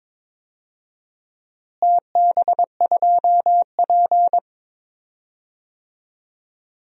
Morse code, heard as a keyed tone, about 22 wpm, 700 Hz. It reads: TB2P
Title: DXCC-100 Call Signs - 4 Characters - Repeated 2x - 22wpm